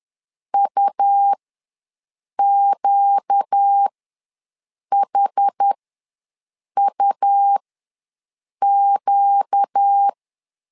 Morse code, heard as a keyed tone, about 10 wpm, spaced as for 6 wpm, 800 Hz. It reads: UQHUQ
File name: LEZIONI / 4 Q ws 10wpm 2306